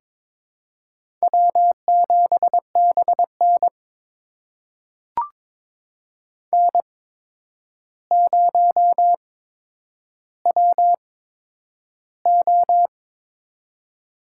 Morse code, heard as a keyed tone, about 22 wpm, spaced as for 9 wpm, 700 Hz. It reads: W7BN N0WO
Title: USA Call Signs - 4 Characters - 8x Character Spacing - 22wpm